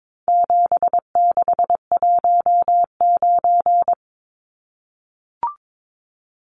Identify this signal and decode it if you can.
Morse code, heard as a keyed tone, about 22 wpm, 700 Hz. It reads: 7619
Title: Numbers - 4-Digit 22wpm